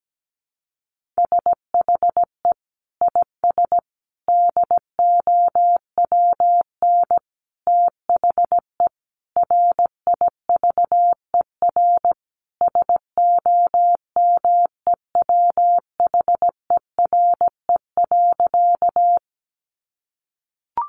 Morse code, heard as a keyed tone, about 17 wpm, 700 Hz. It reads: SHE IS DOWN THE RIVER SOMEWHERE.
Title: Sentences from Top 1400 Words 17wpm